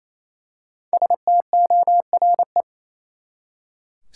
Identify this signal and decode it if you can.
Morse code, heard as a keyed tone, about 28 wpm, 700 Hz. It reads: STORE
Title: Five Letter Words - 28wpm